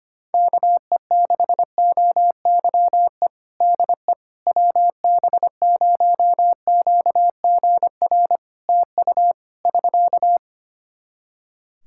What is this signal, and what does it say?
Morse code, heard as a keyed tone, about 25 wpm, 700 Hz. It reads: KE6OYE DE WB0QGR TU <SK>
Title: Practice QSO - Exchange 4 - Ending - 25wpm